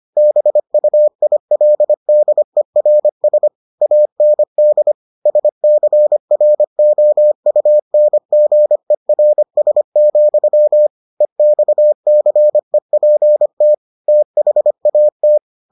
Morse code, heard as a keyed tone, about 25 wpm, 600 Hz. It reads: BUILDERS AND SCROUNGERS, EXCEPT THAT